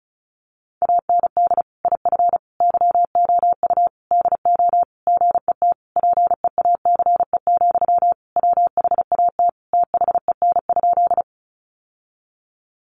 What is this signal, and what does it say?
Morse code, heard as a keyed tone, about 34 wpm, 700 Hz. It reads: AND IF YOU DO GET PEACE, WHAT THEN?
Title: Sentences from Top 1500 Words 35wpm